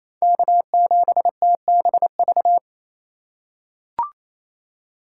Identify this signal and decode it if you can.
Morse code, heard as a keyed tone, about 28 wpm, 700 Hz. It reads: K7TBV
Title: USA Call Signs - 5 Characters - 4x Character Spacing - 28wpm